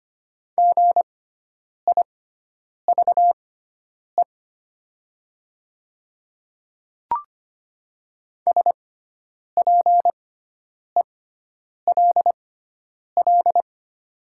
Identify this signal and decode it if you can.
Morse code, heard as a keyed tone, about 25 wpm, spaced as for 9 wpm, 690 Hz. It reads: GIVE SPELL
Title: 6x Character Spacing - Top 300 Words - 25wpm